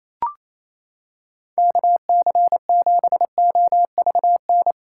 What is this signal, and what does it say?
Morse code, heard as a keyed tone, about 28 wpm, 700 Hz. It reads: KC7OVN
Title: USA Callsign DE Callsign 28wpm